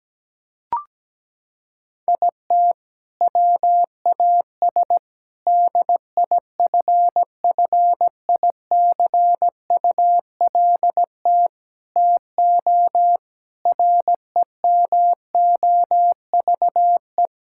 Morse code, heard as a keyed tone, about 17 wpm, 700 Hz. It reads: IT WAS DIFFICULT TO REMOVE
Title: Sentences from Top 1000 Words - Review Part 6 - 17wpm